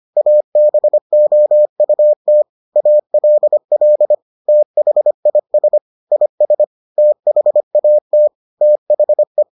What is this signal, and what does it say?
Morse code, heard as a keyed tone, about 25 wpm, 600 Hz. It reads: ABOUT ALL THIS IS THAT THE